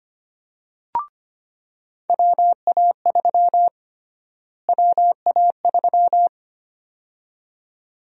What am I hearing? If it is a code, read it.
Morse code, heard as a keyed tone, about 25 wpm, 700 Hz. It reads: WA3 WA3
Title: USA Call Sign Prefixes - Repeated 2x - 25wpm